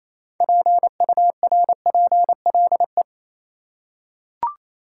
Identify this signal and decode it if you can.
Morse code, heard as a keyed tone, about 27 wpm, 700 Hz. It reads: PURPLE
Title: Six Letter Words - 28wpm